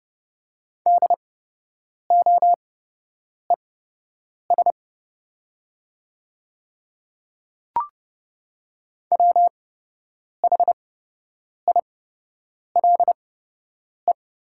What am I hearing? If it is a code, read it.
Morse code, heard as a keyed tone, about 30 wpm, spaced as for 8 wpm, 700 Hz. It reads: DOES WHILE
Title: 8x Character Spacing - Top 300 Words - 30wpm